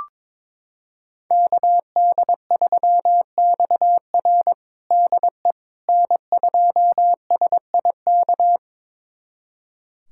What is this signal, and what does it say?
Morse code, heard as a keyed tone, about 22 wpm, 700 Hz. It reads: KD3XR DE N2SIK
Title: USA Callsign DE Callsign - 5 Characters - 22wpm